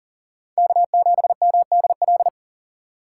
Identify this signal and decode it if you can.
Morse code, heard as a keyed tone, about 40 wpm, 700 Hz. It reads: K7MDL
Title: USA Call Signs - 5 Characters - 4x Character Spacing - 40wpm